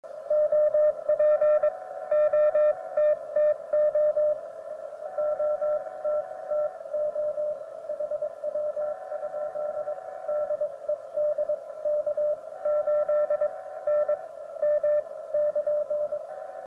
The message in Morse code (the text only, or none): OP OTTO OTTO HW? DE DK8NM <KN>